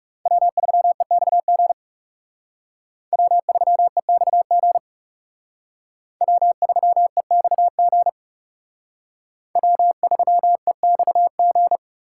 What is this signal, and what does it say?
Morse code, heard as a keyed tone, about 38 wpm, 700 Hz. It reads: W3EXG W3EXG W3EXG W3EXG